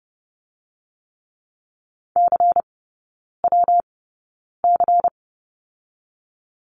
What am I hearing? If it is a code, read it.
Morse code, heard as a keyed tone, about 30 wpm, 700 Hz. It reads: C W C
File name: Lesson 19e - Letter w - Three Characters 30wpm